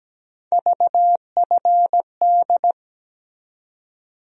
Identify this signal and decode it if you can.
Morse code, heard as a keyed tone, about 17 wpm, 700 Hz. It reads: VFD
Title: USA Call Sign Postfixes - 3 Letters 17wpm